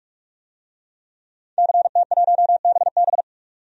Morse code, heard as a keyed tone, about 46 wpm, 700 Hz. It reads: CT1BB
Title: DXCC-100 Call Signs - 5 Characters - 45wpm